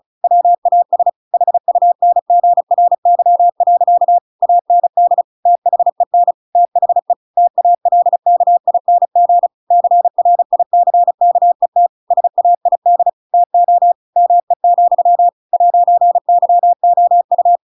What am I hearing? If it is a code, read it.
Morse code, heard as a keyed tone, about 34 wpm, 700 Hz. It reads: WAS HUNGRY. AND THEN THE TALKING CRICKET SAID TO ME, 'YOU
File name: Pinocchio / Pinocchio ch7 35 35 ik1hge